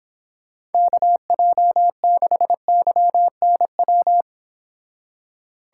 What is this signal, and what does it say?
Morse code, heard as a keyed tone, about 29 wpm, 700 Hz. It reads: KJ6YNW